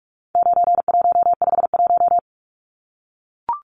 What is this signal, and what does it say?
Morse code, heard as a keyed tone, about 44 wpm, 700 Hz. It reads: 9151
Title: Numbers - 4-Digit 45wpm